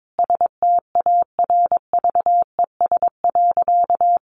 Morse code, heard as a keyed tone, about 22 wpm, 700 Hz. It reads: STARVES.
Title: A Sound Of Thunder 22wpm